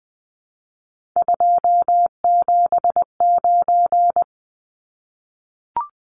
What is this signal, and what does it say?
Morse code, heard as a keyed tone, about 20 wpm, 700 Hz. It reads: 279
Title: Numbers - 3-Digit 20wpm